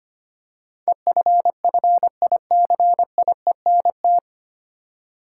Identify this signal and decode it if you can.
Morse code, heard as a keyed tone, about 25 wpm, 700 Hz. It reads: EFFICIENT
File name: Nine Letter Words - 25wpm